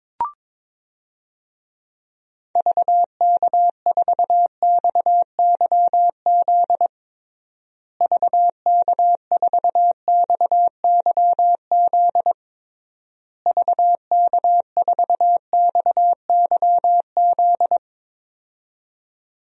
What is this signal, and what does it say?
Morse code, heard as a keyed tone, about 22 wpm, 700 Hz. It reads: VK4XYZ VK4XYZ VK4XYZ
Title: DXCC-100 Call Signs - 6 Characters - Repeated 3x - 22wpm